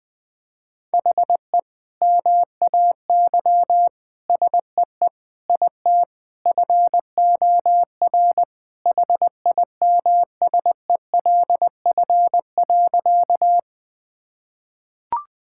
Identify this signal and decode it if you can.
Morse code, heard as a keyed tone, about 20 wpm, 700 Hz. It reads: HE MAY SEE IT FOR HIMSELF.
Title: Sentences from Top 1600 Words 20wpm